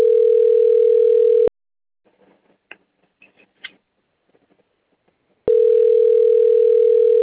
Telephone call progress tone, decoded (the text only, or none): ringback tone